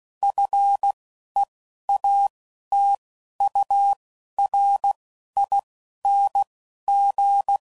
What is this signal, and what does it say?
Morse code, heard as a keyed tone, about 16 wpm, 800 Hz. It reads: F E A T U R I N G
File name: Suffixes / Special INGs / Single INGs